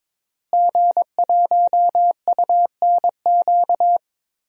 Morse code, heard as a keyed tone, about 22 wpm, 700 Hz. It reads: G1UNQ
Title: DXCC-100 Call Signs - 5 Characters - Repeated 3x - 22wpm